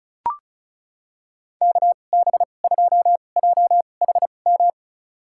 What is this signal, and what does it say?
Morse code, heard as a keyed tone, about 36 wpm, spaced as for 25 wpm, 700 Hz. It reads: KB2JHM
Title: USA Call Signs - 6 Characters - 2x Character Spacing - 35wpm